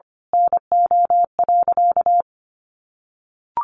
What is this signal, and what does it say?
Morse code, heard as a keyed tone, about 25 wpm, 700 Hz. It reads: NO.
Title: A Sound Of Thunder 25wpm